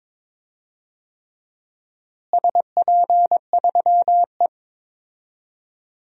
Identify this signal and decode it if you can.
Morse code, heard as a keyed tone, about 22 wpm, 700 Hz. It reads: SP3E
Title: DXCC-100 Call Signs - 4 Characters - Repeated 3x - 22wpm